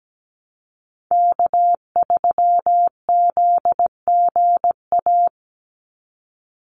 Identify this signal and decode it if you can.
Morse code, heard as a keyed tone, about 17 wpm, 700 Hz. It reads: K3ZGA
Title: Call Signs Repeated 3x - 17wpm